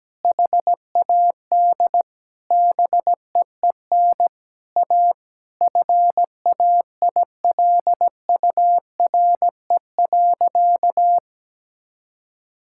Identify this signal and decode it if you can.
Morse code, heard as a keyed tone, about 17 wpm, 700 Hz. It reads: HAD BEEN A FAILURE.
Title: Sentences from Top 1000 Words - Review Part 6 - 17wpm